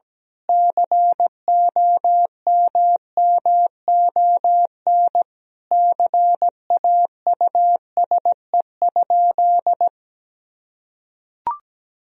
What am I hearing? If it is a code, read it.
Morse code, heard as a keyed tone, about 17 wpm, 700 Hz. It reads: COMMON CAUSE?
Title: Sentences from Top 1000 Words - Review Part 3 - 17wpm